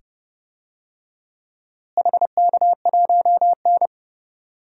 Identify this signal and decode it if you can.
Morse code, heard as a keyed tone, about 30 wpm, 700 Hz. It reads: HK1N
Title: DXCC-100 Call Signs - 4 Characters - Repeated 2x - 30wpm